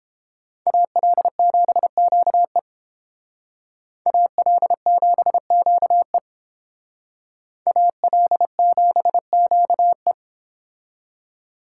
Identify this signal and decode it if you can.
Morse code, heard as a keyed tone, about 30 wpm, 700 Hz. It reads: AL7QE AL7QE AL7QE